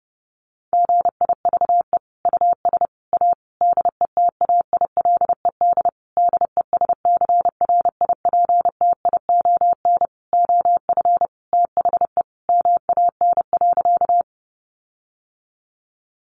Morse code, heard as a keyed tone, about 30 wpm, 700 Hz. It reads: GIVE US A DETAILED DESCRIPTION OF THE MAN.
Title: Sentences from Top 1500 Words 30wpm